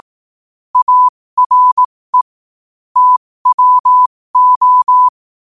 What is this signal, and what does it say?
Morse code, heard as a keyed tone, about 18 wpm, spaced as for 15 wpm, 1000 Hz. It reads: ARE TWO
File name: qotdfiles / 15wpm